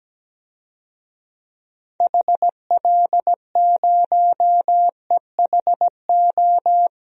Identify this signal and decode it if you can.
Morse code, heard as a keyed tone, about 17 wpm, 700 Hz. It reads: HL0EHO
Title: DXCC-100 Call Signs - 6 Characters - Repeated 3x - 17wpm